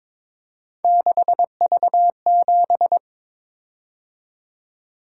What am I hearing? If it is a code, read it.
Morse code, heard as a keyed tone, about 22 wpm, 700 Hz. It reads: 6V7
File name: practice / 22wpm / Three Letter-Number Combinations - 22wpm